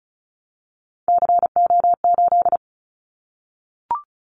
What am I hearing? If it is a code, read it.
Morse code, heard as a keyed tone, about 34 wpm, 700 Hz. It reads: CO8
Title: Callsign Prefixes for Top 100 DXCC - 35wpm